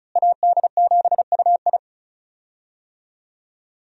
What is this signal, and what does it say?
Morse code, heard as a keyed tone, about 36 wpm, 700 Hz. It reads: AD7UI
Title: USA Call Signs - 5 Characters - Repeated 3x - 35wpm